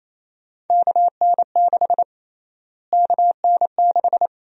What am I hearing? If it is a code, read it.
Morse code, heard as a keyed tone, about 27 wpm, 700 Hz. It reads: KN6 KN6
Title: USA Call Sign Prefixes - Repeated 2x - 28wpm